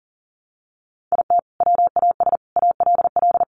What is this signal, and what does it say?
Morse code, heard as a keyed tone, about 40 wpm, 700 Hz. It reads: IT WAS ALL
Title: Sentences from Top 600 Words 40wpm